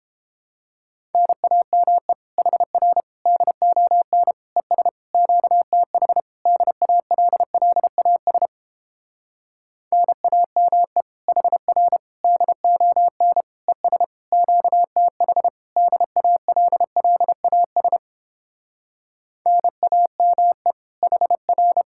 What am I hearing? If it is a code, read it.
Morse code, heard as a keyed tone, about 29 wpm, 700 Hz. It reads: NAME HR DON ES QTH DALLAS NAME HR DON ES QTH DALLAS NAME HR